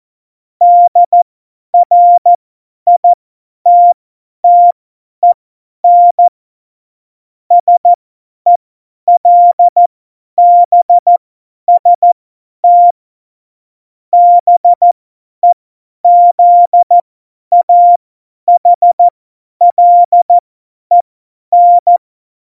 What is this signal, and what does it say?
Morse code, heard as a keyed tone, about 14 wpm, spaced as for 10 wpm, 700 Hz. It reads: DRITTEN SELBST BEZAHLEN
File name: Morsekurs / Witze / Witz 26. September 17, Tempo 50